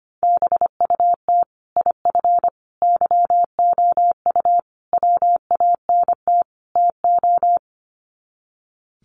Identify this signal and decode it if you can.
Morse code, heard as a keyed tone, about 25 wpm, 700 Hz. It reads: BUT IF YOU WANT TO
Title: Sets of 5 Words Encore2 25wpm